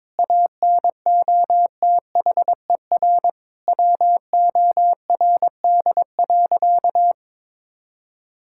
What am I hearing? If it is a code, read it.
Morse code, heard as a keyed tone, about 22 wpm, 700 Hz. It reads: ANOTHER WORD.
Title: Sentences from Top 1000 Words - Review Part 6 - 22wpm